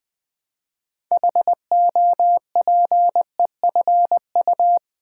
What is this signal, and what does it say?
Morse code, heard as a keyed tone, about 20 wpm, 700 Hz. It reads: HOPEFU
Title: Build A Word - 3 Letters At A Time - 9-Letter Words - 20wpm